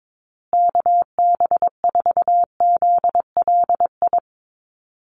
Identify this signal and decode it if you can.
Morse code, heard as a keyed tone, about 22 wpm, 700 Hz. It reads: KB4ZLI
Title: USA Call Signs Repeated 3x 22wpm